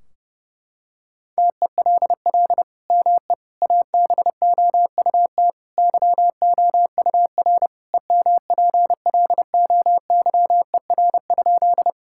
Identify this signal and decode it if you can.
Morse code, heard as a keyed tone, about 30 wpm, 700 Hz. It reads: TELL ME ABOUT YOUR EMPLOYER?
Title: Sentences from Top 1600 Words 30wpm